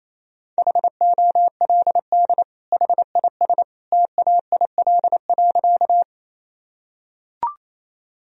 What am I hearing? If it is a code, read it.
Morse code, heard as a keyed tone, about 28 wpm, 700 Hz. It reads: HOLD HIS TAIL.
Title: Sentences from Top 1000 Words - Review Part 3 - 28wpm